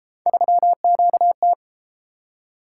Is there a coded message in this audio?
Yes: Morse code from 0.3 to 1.5 s, heard as a keyed tone, about 33 wpm, 700 Hz. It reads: 3QT